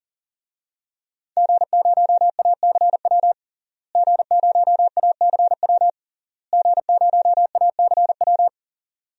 Morse code, heard as a keyed tone, about 37 wpm, 700 Hz. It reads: G0ACW G0ACW G0ACW
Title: DXCC-100 Call Signs - 5 Characters - Repeated 3x - 40wpm